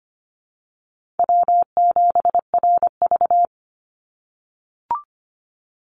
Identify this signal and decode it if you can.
Morse code, heard as a keyed tone, about 25 wpm, 700 Hz. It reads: W7RV
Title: NA SOTA Chaser Callsigns 25wpm